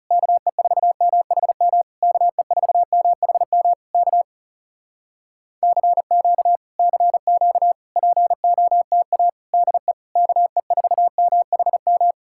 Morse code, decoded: KE4MHM KE4MHM K CQ CQ POTA DE KE4MHM

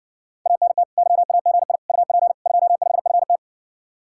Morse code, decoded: ANT YAGI UP 25FT

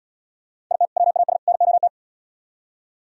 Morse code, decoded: IT WAS NOT

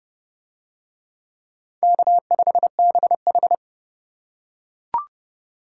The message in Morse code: K5BH

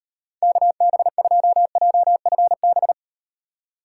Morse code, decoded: KB2JFB

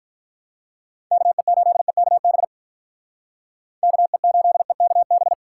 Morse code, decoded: KE8EKB KE8EKB

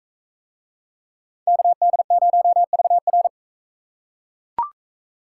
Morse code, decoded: KD0VR

